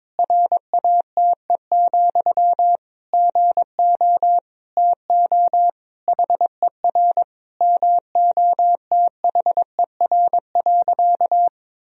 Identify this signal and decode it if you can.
Morse code, heard as a keyed tone, about 22 wpm, 700 Hz. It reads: RATE, GO TO HER MOTHER.
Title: Sentences from Top 1000 Words - Review Part 4 - 22wpm